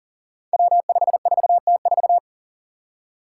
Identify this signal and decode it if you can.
Morse code, heard as a keyed tone, about 42 wpm, 700 Hz. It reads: W54T4
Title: Five Challenging Letter-Number Combinations 40wpm